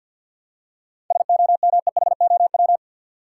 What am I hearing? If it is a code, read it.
Morse code, heard as a keyed tone, about 50 wpm, 700 Hz. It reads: SOMEHOW
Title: Seven Letter Words - 50wpm